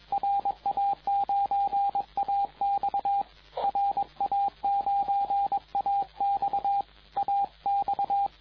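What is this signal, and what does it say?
Morse code, heard as a keyed tone, about 21 wpm, 800 Hz, with radio noise. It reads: RA9AX RA9AX AX
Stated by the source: CW contest exchange in the 2014 Popov Cup VHF contest, RA9AX calling RA9BZ, recorded by the TR4W logger.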